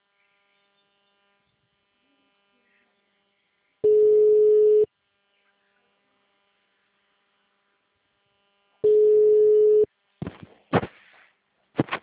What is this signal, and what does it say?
Ringback tone (Europe, 425 Hz).